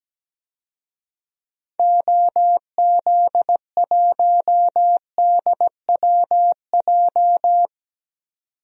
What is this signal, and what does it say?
Morse code, heard as a keyed tone, about 17 wpm, 700 Hz. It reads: OZ1DWJ
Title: DXCC-100 Call Signs - 6 Characters - 17wpm